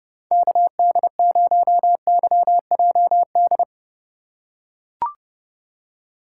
Morse code, heard as a keyed tone, about 32 wpm, 700 Hz. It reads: KD0YJD
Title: USA Call Signs - 6 Characters - 4x Character Spacing - 30wpm